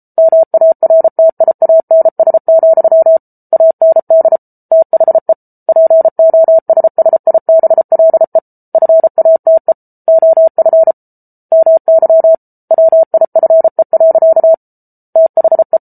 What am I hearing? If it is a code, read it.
Morse code, heard as a keyed tone, about 34 wpm, 650 Hz. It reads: MARTIANS, AND THE POSSIBLE FATE OF MY WIFE. THE